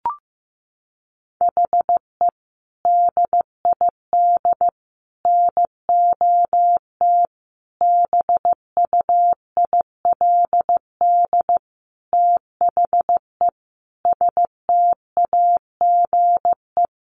Morse code, heard as a keyed tone, about 15 wpm, 700 Hz. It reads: HE DID NOT BUILD THE STAGE
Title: Sentences from Top 1200 Words 15wpm